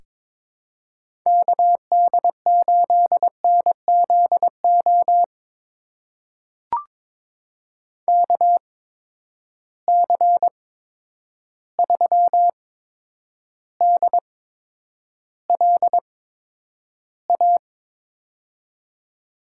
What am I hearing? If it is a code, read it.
Morse code, heard as a keyed tone, about 23 wpm, spaced as for 10 wpm, 700 Hz. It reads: KD8NZO KC3DLA
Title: USA Call Signs - 6 Characters - 8x Character Spacing - 22wpm